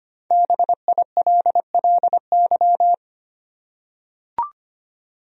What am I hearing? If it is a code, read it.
Morse code, heard as a keyed tone, about 25 wpm, 700 Hz. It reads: BILLY
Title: USA Common Names - 25wpm